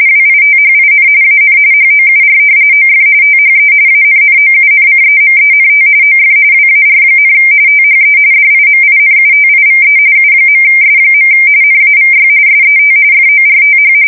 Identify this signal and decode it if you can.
RTTY, about 45.45 baud. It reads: WITHIN AN HOUR OF THE ANNOUNCEMENT OF WAR, THE DIMINUTIVE U.S. NAVY SET FORTH
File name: fsk170 / usnavy2